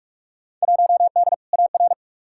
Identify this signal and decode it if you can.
Morse code, heard as a keyed tone, about 46 wpm, 700 Hz. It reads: 1D AR
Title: Field Day Exchanges Encore - 45wpm